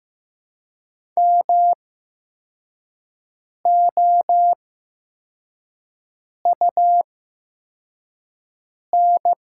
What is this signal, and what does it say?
Morse code, heard as a keyed tone, about 15 wpm, spaced as for 4 wpm, 700 Hz. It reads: MOUN
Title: 8x Character Spacing - Top 300 Words - 15wpm